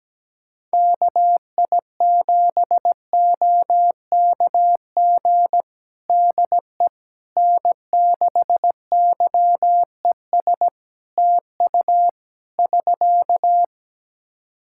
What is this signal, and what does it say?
Morse code, heard as a keyed tone, about 17 wpm, 700 Hz. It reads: KI7OKG DE N6YES TU <SK>
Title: Practice QSO - Exchange 4 - Ending - 17wpm